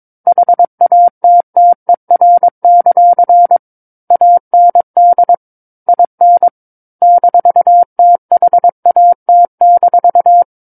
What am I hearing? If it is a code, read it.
Morse code, heard as a keyed tone, about 22 wpm, 700 Hz. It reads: HATTER; AND IN -THAT-
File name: 06-Chapter-06 22 WPM